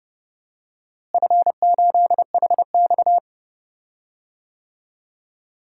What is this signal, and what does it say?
Morse code, heard as a keyed tone, about 30 wpm, 700 Hz. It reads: F8HX
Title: DXCC-100 Call Signs - 4 Characters - Repeated 2x - 30wpm